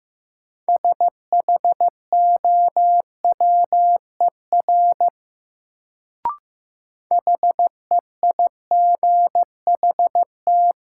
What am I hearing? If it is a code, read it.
Morse code, heard as a keyed tone, about 15 wpm, 700 Hz. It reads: SHOWER HEIGHT
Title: Sentences from Top 1600 Words - New Word Drill - 15wpm